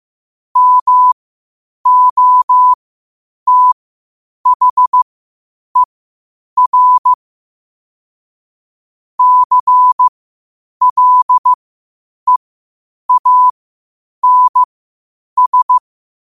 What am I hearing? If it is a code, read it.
Morse code, heard as a keyed tone, about 15 wpm, spaced as for 8 wpm, 1000 Hz. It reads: MOTHER CLEANS
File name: qotdfiles / 8wpm